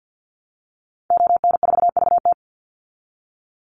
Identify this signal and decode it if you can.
Morse code, heard as a keyed tone, about 49 wpm, 700 Hz. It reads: ON4VT
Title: Call Signs Repeated 3x - 50wpm